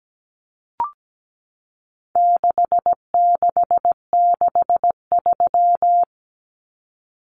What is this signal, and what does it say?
Morse code, heard as a keyed tone, about 17 wpm, 700 Hz. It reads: 6663
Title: Numbers - 4-Digit 17wpm